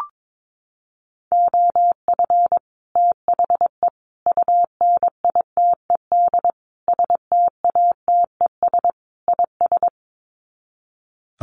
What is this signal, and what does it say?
Morse code, heard as a keyed tone, about 22 wpm, 700 Hz. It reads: OF THE UNITED STATES IS
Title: Sets of 5 Words Encore2 22wpm